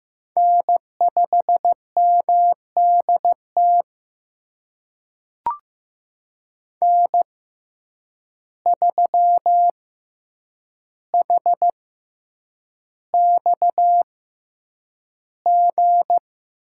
Morse code, heard as a keyed tone, about 15 wpm, spaced as for 8 wpm, 700 Hz. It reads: N5MDT N3HXG
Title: USA Call Signs - 5 Characters - 6x Character Spacing - 15wpm